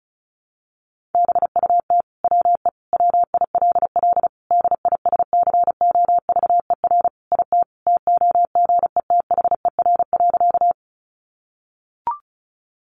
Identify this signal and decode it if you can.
Morse code, heard as a keyed tone, about 34 wpm, 700 Hz. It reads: BUT WE WILL DISCOVER IT TOGETHER.
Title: Sentences from Top 1100 Words 35wpm